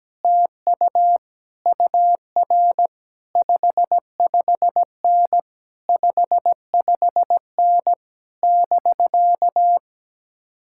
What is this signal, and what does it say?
Morse code, heard as a keyed tone, about 17 wpm, 700 Hz. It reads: TU UR 55N 55N <BK>
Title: Speed-Racing Common QSO Phrases - 17wpm